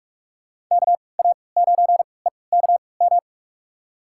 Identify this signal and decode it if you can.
Morse code, heard as a keyed tone, about 46 wpm, spaced as for 26 wpm, 700 Hz. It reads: KA9EKM
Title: USA Call Signs - 3x Character Spacing - 45wpm